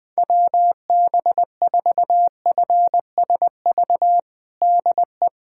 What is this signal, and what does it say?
Morse code, heard as a keyed tone, about 20 wpm, 700 Hz. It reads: WB4FSV DE
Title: USA Callsign DE Callsign - 6 Characters - 20wpm